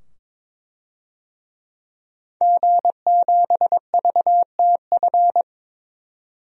Morse code, heard as a keyed tone, about 22 wpm, 700 Hz. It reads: G7VTF
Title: DXCC-100 Call Signs - 5 Characters - 22wpm